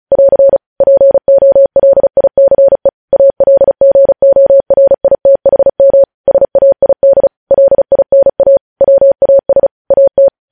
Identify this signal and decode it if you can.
Morse code, heard as a keyed tone, about 36 wpm, 550 Hz. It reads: <AR> POLICE ALGORITHM SAID LINA WAS AT
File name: podcast / News headlines-1x-35x35